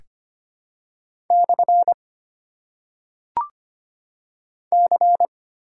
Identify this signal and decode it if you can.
Morse code, heard as a keyed tone, about 25 wpm, 690 Hz. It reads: / C
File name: Top 90 QSO Elements - 8x Character Spacing - 25wpm